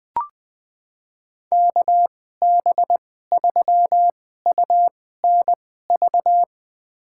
Morse code, heard as a keyed tone, about 20 wpm, spaced as for 14 wpm, 700 Hz. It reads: KB3UNV
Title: USA Call Signs - 2x Character Spacing - 20wpm